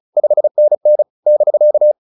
Morse code, heard as a keyed tone, about 36 wpm, 600 Hz. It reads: 5NN <BK>